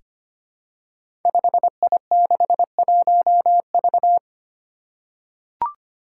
Morse code, heard as a keyed tone, about 25 wpm, 700 Hz. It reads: 5I61V
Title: Five Challenging Letter-Number Combinations 25wpm